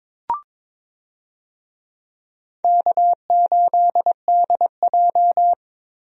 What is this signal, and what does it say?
Morse code, heard as a keyed tone, about 22 wpm, 700 Hz. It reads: K8DJ
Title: USA Call Signs - 4 Characters - Repeated 2x - 22wpm